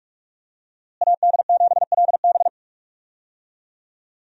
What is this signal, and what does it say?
Morse code, heard as a keyed tone, about 46 wpm, 700 Hz. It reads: AD7LB
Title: USA Call Signs - 5 Characters - Repeated 3x - 45wpm